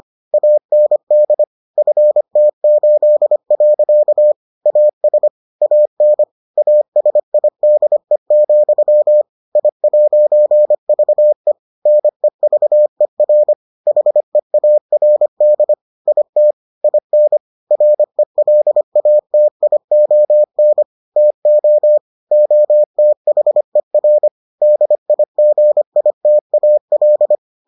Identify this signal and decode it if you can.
Morse code, heard as a keyed tone, about 25 wpm, 600 Hz. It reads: AND FT8. AS AN ASIDE, I'VE NEVER HEARD IT IN RELATION TO OTHER DIGITAL